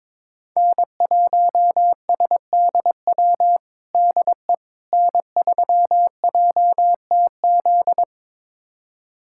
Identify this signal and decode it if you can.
Morse code, heard as a keyed tone, about 22 wpm, 700 Hz. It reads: N1SDW DE N3JTZ
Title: USA Callsign DE Callsign - 5 Characters - 22wpm